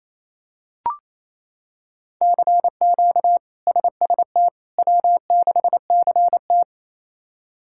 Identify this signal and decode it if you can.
Morse code, heard as a keyed tone, about 28 wpm, 700 Hz. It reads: CQ SST W6CT
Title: K1USN SST QSO Practice - 28wpm